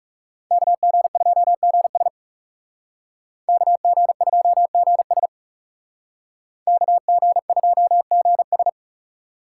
Morse code, heard as a keyed tone, about 37 wpm, 700 Hz. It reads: KG2GS KG2GS KG2GS
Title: Speed Racing - USA 2x2 Call Signs 30wpm